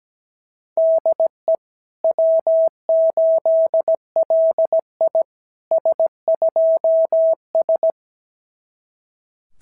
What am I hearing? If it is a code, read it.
Morse code, heard as a keyed tone, about 16 wpm, 650 Hz. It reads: DE W8LI S2S
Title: SOTA QSO Practice 17wpm